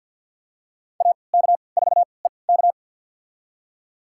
Morse code, heard as a keyed tone, about 47 wpm, spaced as for 28 wpm, 700 Hz. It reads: AK4EK
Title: USA Call Signs - 3x Character Spacing - 50wpm